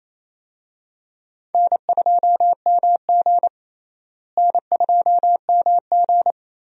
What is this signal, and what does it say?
Morse code, heard as a keyed tone, about 28 wpm, 700 Hz. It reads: N2MG N2MG
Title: USA Call Signs - 4 Characters - Repeated 3x - 28wpm